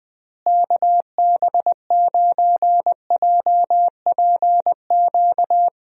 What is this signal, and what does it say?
Morse code, heard as a keyed tone, about 20 wpm, 700 Hz. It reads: KB9JPQ
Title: USA Call Signs - 6 Characters - Repeated 3x - 20wpm